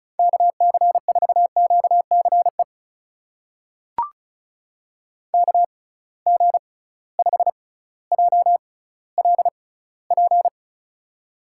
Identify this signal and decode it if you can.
Morse code, heard as a keyed tone, about 34 wpm, spaced as for 18 wpm, 700 Hz. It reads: KC4QCE KG5JLP
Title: USA Call Signs - 6 Characters - 6x Character Spacing - 35wpm